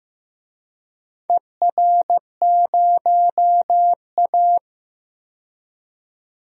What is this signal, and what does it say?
Morse code, heard as a keyed tone, about 15 wpm, 700 Hz. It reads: ER0A
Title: DXCC-100 Call Signs - 4 Characters - 15wpm